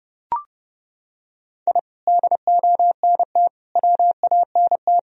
Morse code, heard as a keyed tone, about 30 wpm, 700 Hz. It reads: I DONT WANT
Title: Sentences from Top 1000 Words - Review Part 1 - 30wpm